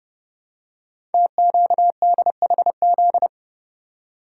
Morse code, heard as a keyed tone, about 29 wpm, 700 Hz. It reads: TQDHZ
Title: Code Groups - 5 Letter:Numbers - 30wpm